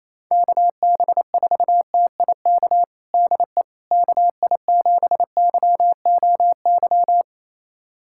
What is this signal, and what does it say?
Morse code, heard as a keyed tone, about 28 wpm, 700 Hz. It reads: KB4TIK DE KI7YOY
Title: USA Callsign DE Callsign - 6 Characters - 28wpm